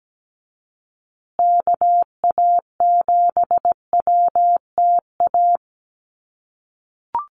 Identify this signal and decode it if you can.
Morse code, heard as a keyed tone, about 17 wpm, 700 Hz. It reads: KA7WTA
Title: USA Call Signs Repeated 3x 17wpm